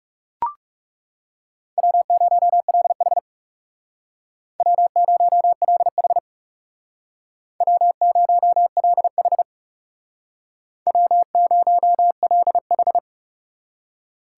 Morse code, heard as a keyed tone, about 37 wpm, 700 Hz. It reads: W0LH W0LH W0LH W0LH
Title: Speed Racing - USA 1x2 Call Signs 30wpm